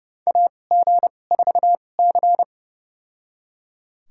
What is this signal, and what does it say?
Morse code, heard as a keyed tone, about 31 wpm, spaced as for 22 wpm, 700 Hz. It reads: AG4C
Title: USA Call Signs - 2x Character Spacing - 30wpm